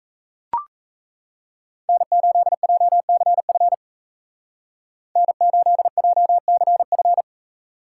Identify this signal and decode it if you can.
Morse code, heard as a keyed tone, about 34 wpm, 700 Hz. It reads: N8JCF N8JCF
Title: Speed Racing - USA 1x3 Call Signs 28wpm